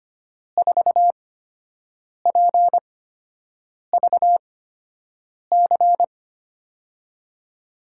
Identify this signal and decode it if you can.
Morse code, heard as a keyed tone, about 25 wpm, spaced as for 7 wpm, 700 Hz. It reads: 4PVC
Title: USA Call Signs - 6 Characters - 8x Character Spacing - 25wpm